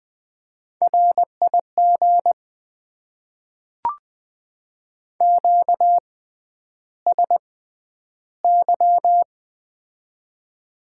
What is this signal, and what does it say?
Morse code, heard as a keyed tone, about 20 wpm, spaced as for 10 wpm, 700 Hz. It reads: RIG QSY
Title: Top 90 QSO Elements - 6x Character Spacing - 20wpm